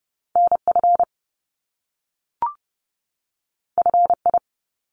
Morse code, heard as a keyed tone, about 30 wpm, 700 Hz. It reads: NF FI
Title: Top Two Letter Combinations in Words 30wpm